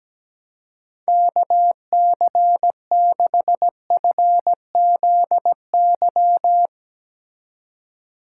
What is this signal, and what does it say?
Morse code, heard as a keyed tone, about 18 wpm, 700 Hz. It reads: KC6FZY
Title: Speed Racing - USA Call Signs 17wpm